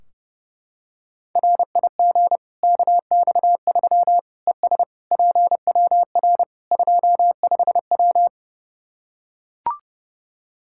Morse code, heard as a keyed tone, about 30 wpm, 700 Hz. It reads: RIG KX3 ES PWR 25W
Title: K1USN SST QSO Practice 30wpm